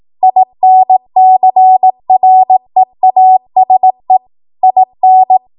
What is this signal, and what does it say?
Morse code, heard as a keyed tone, about 18 wpm, 750 Hz. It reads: INCREASE IN
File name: Morse / 18WPM